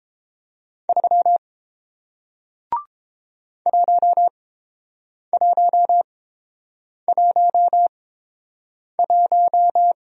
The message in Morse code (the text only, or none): 3 1111